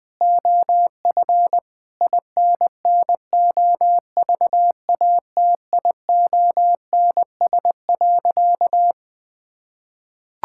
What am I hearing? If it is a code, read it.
Morse code, heard as a keyed tone, about 20 wpm, 700 Hz. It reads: OF INNOVATIONS.
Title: Sentences with 7-Word Phrases - 20wpm